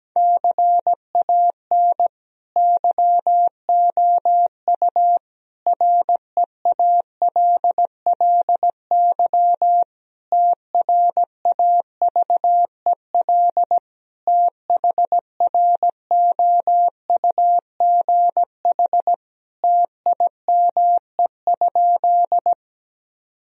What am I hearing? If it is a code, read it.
Morse code, heard as a keyed tone, about 17 wpm, 700 Hz. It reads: CAN YOU REALLY TRAVEL THROUGH TIME?
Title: Sentences from Top 1000 Words - Review Part 2 - 17wpm